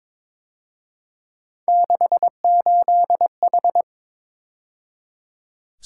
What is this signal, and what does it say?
Morse code, heard as a keyed tone, about 26 wpm, 700 Hz. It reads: 68H
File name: Speed-Racing - Three Letter-Number Combinations - 22wpm